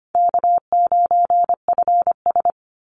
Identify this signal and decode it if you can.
Morse code, heard as a keyed tone, about 25 wpm, 700 Hz. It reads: K9FS
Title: Easy Call Signs Repeated 2x - 25wpm